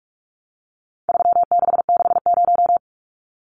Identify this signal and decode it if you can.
Morse code, heard as a keyed tone, about 44 wpm, 700 Hz. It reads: 3660
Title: Numbers - 4-Digit 45wpm